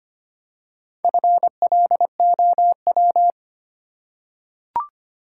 Morse code, heard as a keyed tone, about 25 wpm, 700 Hz. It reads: FLOW
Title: Four Letter Words - 25wpm